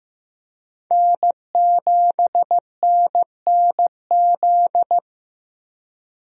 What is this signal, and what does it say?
Morse code, heard as a keyed tone, about 15 wpm, 700 Hz. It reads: N7NNZ
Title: USA Call Signs - 5 Characters - 15wpm